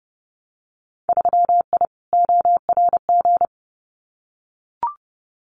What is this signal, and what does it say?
Morse code, heard as a keyed tone, about 30 wpm, 700 Hz. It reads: 3I ORG
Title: Winter Field Day Exchanges Encore 30wpm